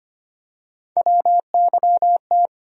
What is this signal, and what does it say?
Morse code, heard as a keyed tone, about 25 wpm, 700 Hz. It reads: WYT
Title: USA Call Sign Postfixes - 3 Letters 25wpm